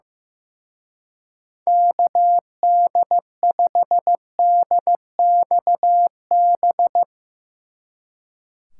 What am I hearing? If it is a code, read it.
Morse code, heard as a keyed tone, about 16 wpm, 700 Hz. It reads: KD5DXB
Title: Speed Racing - USA 2x3 Call Signs 15wpm